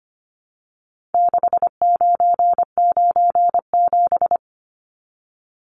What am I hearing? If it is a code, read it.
Morse code, heard as a keyed tone, about 25 wpm, 700 Hz. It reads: 6997
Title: Numbers - 4-Digit 25wpm